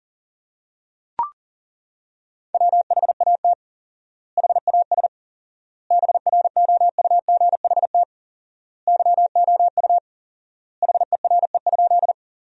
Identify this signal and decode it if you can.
Morse code, heard as a keyed tone, about 40 wpm, 700 Hz. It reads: WHAT HAS BROUGHT YOU HERE?
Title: 4x Word-Spacing - Sentences from Top 500 Words - 40wpm